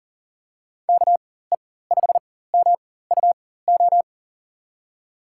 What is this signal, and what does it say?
Morse code, heard as a keyed tone, about 40 wpm, spaced as for 19 wpm, 700 Hz. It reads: KE5MUO
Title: USA Call Signs - 6 Characters - 4x Character Spacing - 40wpm